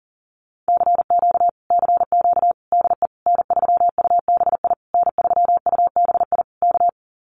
Morse code, heard as a keyed tone, about 40 wpm, 700 Hz. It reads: CQ CQ DE N3UBI N3UBI K
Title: Common QSO Phrases Encore 40wpm